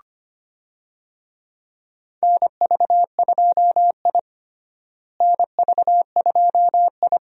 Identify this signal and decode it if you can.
Morse code, heard as a keyed tone, about 25 wpm, 700 Hz. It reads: NV2I NV2I
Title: USA Call Signs - 4 Characters - Repeated 2x - 25wpm